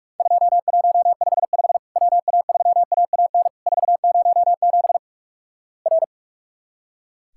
Morse code, heard as a keyed tone, about 46 wpm, 700 Hz. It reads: 2155 WA3AAN 407 R
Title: Worked All Europe DX Contest - QTC Exchange - 45wpm